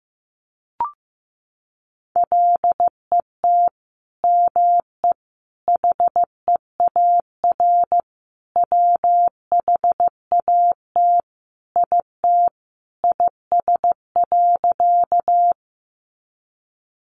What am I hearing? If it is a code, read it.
Morse code, heard as a keyed tone, about 15 wpm, 700 Hz. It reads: LET ME HEAR WHAT IT IS.
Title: Sentences from Top 300 Words 15wpm